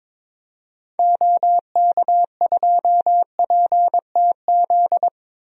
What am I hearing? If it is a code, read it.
Morse code, heard as a keyed tone, about 22 wpm, 700 Hz. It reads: OK2PTZ
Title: DXCC-100 Call Signs - 6 Characters - 22wpm